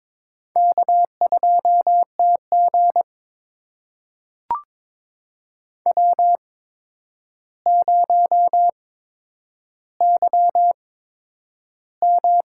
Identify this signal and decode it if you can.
Morse code, heard as a keyed tone, about 22 wpm, spaced as for 9 wpm, 700 Hz. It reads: K2TG W0YM
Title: USA Call Signs - 4 Characters - 8x Character Spacing - 22wpm